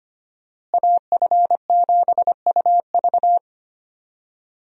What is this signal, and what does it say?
Morse code, heard as a keyed tone, about 25 wpm, 700 Hz. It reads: AF7UV
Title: USA Call Signs - 5 Characters - Repeated 2x - 25wpm